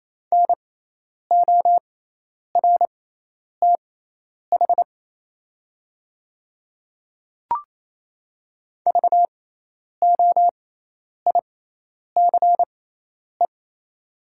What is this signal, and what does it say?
Morse code, heard as a keyed tone, about 28 wpm, spaced as for 10 wpm, 700 Hz. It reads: NORTH VOICE
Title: 6x Character Spacing - Top 500 Words - 28wpm